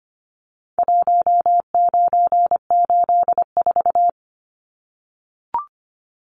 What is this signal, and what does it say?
Morse code, heard as a keyed tone, about 25 wpm, 700 Hz. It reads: 1984
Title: Numbers - Contest Style - 4-Digit 25wpm